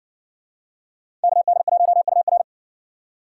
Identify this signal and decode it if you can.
Morse code, heard as a keyed tone, about 55 wpm, 700 Hz. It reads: KD1UR